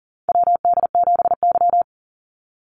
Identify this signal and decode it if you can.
Morse code, heard as a keyed tone, about 40 wpm, 700 Hz. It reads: WD7Y
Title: NA SOTA Chaser Callsigns 40wpm